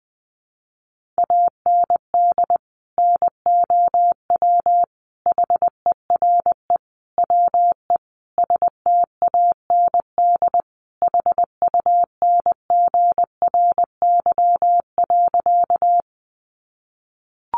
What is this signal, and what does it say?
Morse code, heard as a keyed tone, about 20 wpm, 700 Hz. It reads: AND NOW HERE WE STAND HUNGRY.
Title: Sentences from Top 1400 Words 20wpm